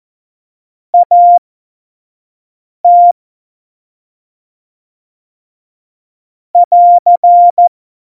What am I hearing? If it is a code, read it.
Morse code, heard as a keyed tone, about 14 wpm, spaced as for 5 wpm, 700 Hz. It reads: AT <AR>
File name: Morsekurs / Witze / Witz 09. April 2019, 25 F70